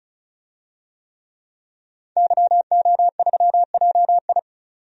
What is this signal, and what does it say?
Morse code, heard as a keyed tone, about 36 wpm, 700 Hz. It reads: YO3JI
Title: DXCC-100 Call Signs - 5 Characters - 35wpm